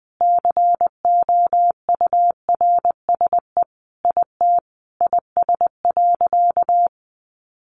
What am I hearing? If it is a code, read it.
Morse code, heard as a keyed tone, about 20 wpm, 700 Hz. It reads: COURSE IT IS.
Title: Sentences from Top 500 Words 20wpm